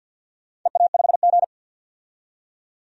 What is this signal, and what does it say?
Morse code, heard as a keyed tone, about 50 wpm, 700 Hz. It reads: EA5G
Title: DXCC-100 Call Signs - 4 Characters - 50wpm